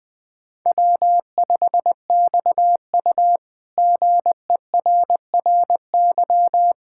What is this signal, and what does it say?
Morse code, heard as a keyed tone, about 20 wpm, 700 Hz. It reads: W5XU GERRY
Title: K1USN SST QSO Practice - 20wpm